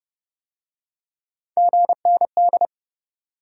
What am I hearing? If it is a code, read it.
Morse code, heard as a keyed tone, about 28 wpm, 700 Hz. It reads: GND